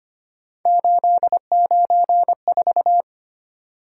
Morse code, heard as a keyed tone, about 25 wpm, 700 Hz. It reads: 894
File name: Three Letter-Number Combinations - 25wpm